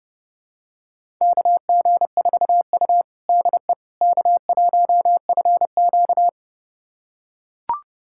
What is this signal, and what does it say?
Morse code, heard as a keyed tone, about 30 wpm, 700 Hz. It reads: KG4U DE K1FQ
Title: USA Callsign DE Callsign - 4 Characters - 30wpm